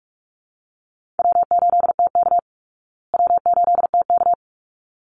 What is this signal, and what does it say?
Morse code, heard as a keyed tone, about 44 wpm, 700 Hz. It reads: W8TK W8TK
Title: Easy Call Signs Repeated 2x - 45wpm